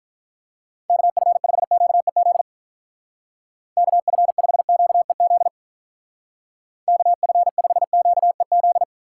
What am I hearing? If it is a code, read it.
Morse code, heard as a keyed tone, about 43 wpm, 700 Hz. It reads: KF5QEZ KF5QEZ KF5QEZ
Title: Speed Racing - USA 2x3 Call Signs 35wpm